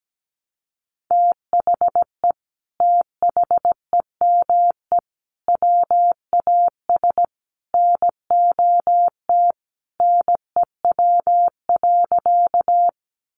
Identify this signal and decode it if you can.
Morse code, heard as a keyed tone, about 17 wpm, 700 Hz. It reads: THE THEME WAS NOT NEW.
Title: Sentences from Top 1100 Words 17wpm